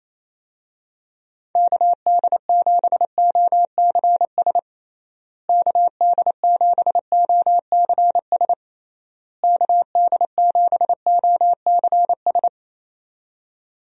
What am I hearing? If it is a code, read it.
Morse code, heard as a keyed tone, about 27 wpm, 700 Hz. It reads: KD7OCS KD7OCS KD7OCS
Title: USA Call Signs - 6 Characters - Repeated 3x - 28wpm